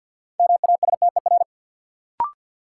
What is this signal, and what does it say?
Morse code, heard as a keyed tone, about 50 wpm, 700 Hz. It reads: MASTER
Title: Sentences from Top 1300 Words - New Word Drill - 50wpm